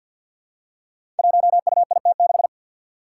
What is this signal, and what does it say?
Morse code, heard as a keyed tone, about 46 wpm, 700 Hz. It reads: 1UIT6
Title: Five Challenging Letter-Number Combinations 50wpm